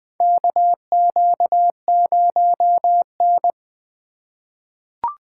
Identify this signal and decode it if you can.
Morse code, heard as a keyed tone, about 20 wpm, 700 Hz. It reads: KQ0N
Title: USA Call Signs - 6x Character Spacing - 20wpm